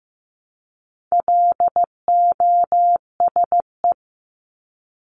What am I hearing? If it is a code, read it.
Morse code, heard as a keyed tone, about 15 wpm, 700 Hz. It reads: LOSE
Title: Top 1001-1250 Words 15wpm